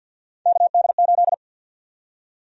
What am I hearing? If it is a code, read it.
Morse code, heard as a keyed tone, about 53 wpm, 700 Hz. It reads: KD8
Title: USA Call Sign Prefixes - 50wpm